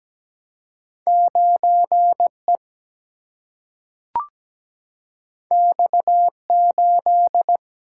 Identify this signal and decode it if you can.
Morse code, heard as a keyed tone, about 17 wpm, 700 Hz. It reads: 9E X8
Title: Single-Letter Single-Number 17wpm